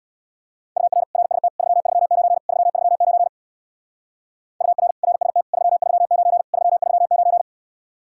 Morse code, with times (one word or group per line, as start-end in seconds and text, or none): 0.8-1.0 UR
1.1-1.5 RST
1.6-2.4 339
2.5-3.3 339
4.6-4.9 UR
5.0-5.4 RST
5.5-6.4 339
6.5-7.4 339